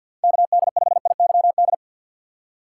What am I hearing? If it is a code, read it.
Morse code, heard as a keyed tone, about 50 wpm, 700 Hz. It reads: KD5IYD
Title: USA Call Signs - 6 Characters - 4x Character Spacing - 50wpm